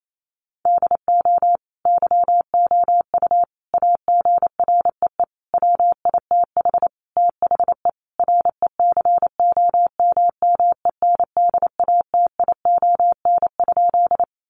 Morse code, heard as a keyed tone, about 28 wpm, 700 Hz. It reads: DO YOU AGREE WITH THE RECOMMENDATION?
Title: Sentences from Top 1400 Words 28wpm